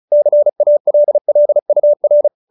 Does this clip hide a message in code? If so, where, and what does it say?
Yes: Morse code from 0.1 to 2.3 s, heard as a keyed tone, about 34 wpm, 590 Hz. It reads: CALL UR